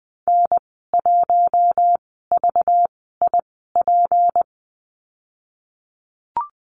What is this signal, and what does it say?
Morse code, heard as a keyed tone, about 20 wpm, 700 Hz. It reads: N 1 V I P
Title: 231 Lesson 37g - Letter x - Call Signs 20wpm